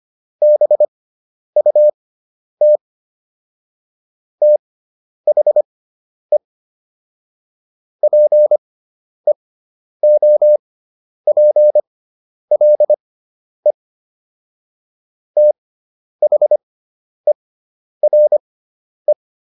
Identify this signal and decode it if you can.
Morse code, heard as a keyed tone, about 25 wpm, spaced as for 10 wpm, 600 Hz. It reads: BUT THE PEOPLE THERE